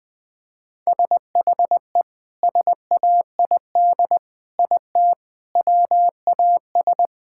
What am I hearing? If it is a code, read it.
Morse code, heard as a keyed tone, about 21 wpm, 700 Hz. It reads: SHE SAID IT WAS
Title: Sentences with 2-Word Phrases - 20wpm